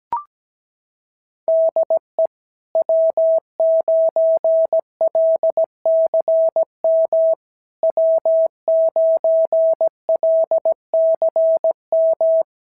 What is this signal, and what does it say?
Morse code, heard as a keyed tone, about 17 wpm, 650 Hz. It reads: DE W9LCM W9LCM
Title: SOTA QSO Practice 17wpm